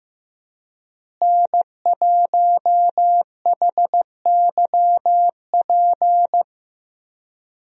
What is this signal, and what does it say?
Morse code, heard as a keyed tone, about 15 wpm, 700 Hz. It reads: N1HYP
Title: USA Call Signs - 5 Characters - Repeated 2x - 15wpm